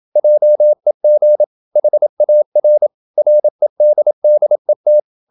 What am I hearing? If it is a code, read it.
Morse code, heard as a keyed tone, about 27 wpm, 600 Hz. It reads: JEG HAR REDDET
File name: Ferie QRQ 27wpm